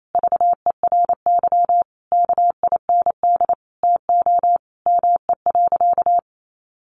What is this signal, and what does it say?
Morse code, heard as a keyed tone, about 28 wpm, 700 Hz. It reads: VERY KIND TO ME.
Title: Sentences from Top 200 Words 28wpm